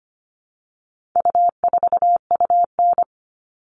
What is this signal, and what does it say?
Morse code, heard as a keyed tone, about 25 wpm, 700 Hz. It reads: U4UN
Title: All Cut Numbers - 4-Digit 25wpm